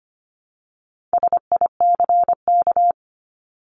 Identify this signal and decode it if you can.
Morse code, heard as a keyed tone, about 25 wpm, 700 Hz. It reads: SICK
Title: Top 1501-1750 Words 25wpm